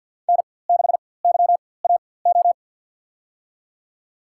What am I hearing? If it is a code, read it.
Morse code, heard as a keyed tone, about 50 wpm, spaced as for 23 wpm, 690 Hz. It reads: N6YAO
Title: USA Call Signs - 5 Characters - 4x Character Spacing - 50wpm